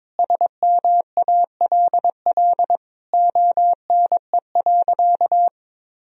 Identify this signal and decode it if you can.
Morse code, heard as a keyed tone, about 21 wpm, 700 Hz. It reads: SMALL ONE.
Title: Sentences from Top 1000 Words - Review Part 1 - 22wpm